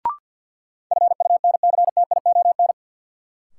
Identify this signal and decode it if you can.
Morse code, heard as a keyed tone, about 53 wpm, 700 Hz. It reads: FUNCTION
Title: Sentences from Top 1100 Words - New Word Drill - 50wpm